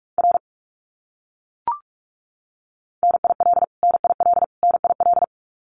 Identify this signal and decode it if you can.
Morse code, heard as a keyed tone, about 44 wpm, 700 Hz. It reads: R NIL NIL NIL
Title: Abbreviations Repeated 3x - 45wpm